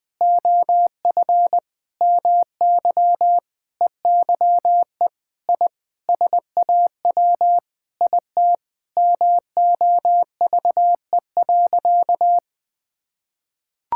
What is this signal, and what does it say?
Morse code, heard as a keyed tone, about 20 wpm, 700 Hz. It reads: OF MY EYE I SAW IT MOVE.
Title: Sentences with 8-Word Phrases - 20wpm